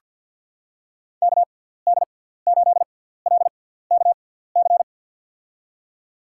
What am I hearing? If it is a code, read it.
Morse code, heard as a keyed tone, about 47 wpm, spaced as for 17 wpm, 700 Hz. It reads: KD8LKC
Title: USA Call Signs - 6 Characters - 6x Character Spacing - 50wpm